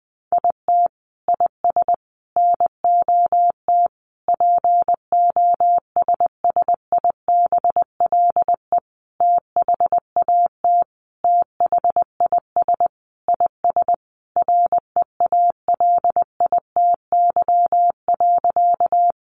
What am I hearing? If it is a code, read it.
Morse code, heard as a keyed tone, about 19 wpm, 700 Hz. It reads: IT IS NOT POSSIBLE THAT THIS IS REALITY.
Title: Sentences from Top 1200 Words 20wpm